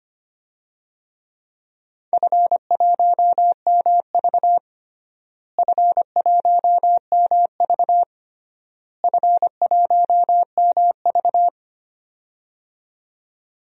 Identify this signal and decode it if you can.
Morse code, heard as a keyed tone, about 25 wpm, 700 Hz. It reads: F1MV F1MV F1MV
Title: DXCC-100 Call Signs - 4 Characters - Repeated 3x - 25wpm